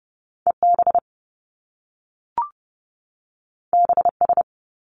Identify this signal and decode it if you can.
Morse code, heard as a keyed tone, about 30 wpm, 700 Hz. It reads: EB BS
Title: Top Two Letter Combinations in Words 30wpm